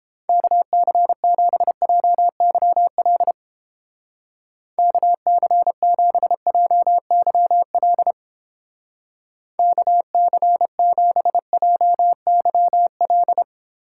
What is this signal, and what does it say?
Morse code, heard as a keyed tone, about 29 wpm, 700 Hz. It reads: KC7JYL KC7JYL KC7JYL